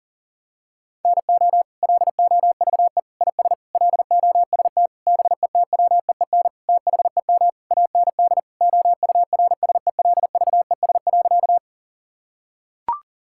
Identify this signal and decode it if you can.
Morse code, heard as a keyed tone, about 40 wpm, 700 Hz. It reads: NO LOVE IS LOST BETWEEN THEM AND OURSELVES.
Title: Sentences from Top 1600 Words 40wpm